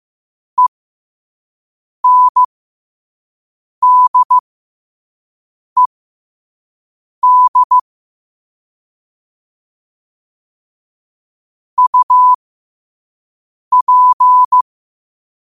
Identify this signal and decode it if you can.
Morse code, heard as a keyed tone, about 15 wpm, spaced as for 5 wpm, 1000 Hz. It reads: ENDED UP
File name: qotdfiles / 5wpm